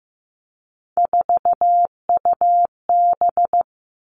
Morse code, heard as a keyed tone, about 15 wpm, 700 Hz. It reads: 4UB
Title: Three Challenging Letter-Number Combinations 15wpm